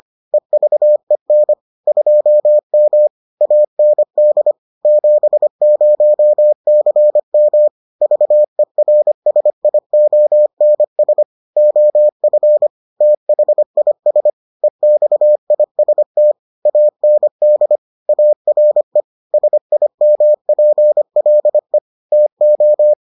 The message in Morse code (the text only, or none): EVEN 2M AND 70CM VERSIONS OF THIS EXIST AND ARE SIMPLE TO